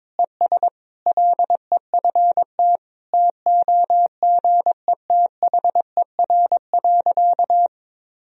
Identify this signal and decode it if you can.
Morse code, heard as a keyed tone, about 22 wpm, 700 Hz. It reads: ES LEFT TOGETHER.
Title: Sentences from Top 400 Words and Abbreviations 22wpm